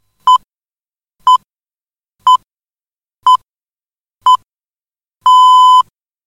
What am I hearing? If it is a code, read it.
Radio time signal (time pips). It is heard from 0.3 to 5.8 s.